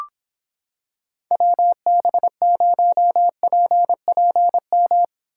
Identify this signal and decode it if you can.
Morse code, heard as a keyed tone, about 26 wpm, 700 Hz. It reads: WB0PPM